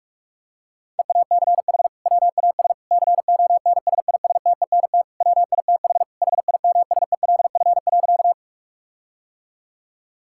Morse code, heard as a keyed tone, about 42 wpm, 700 Hz. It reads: EACH WAS CONSISTENT WITH HIMSELF.
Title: Sentences from Top 1600 Words 45wpm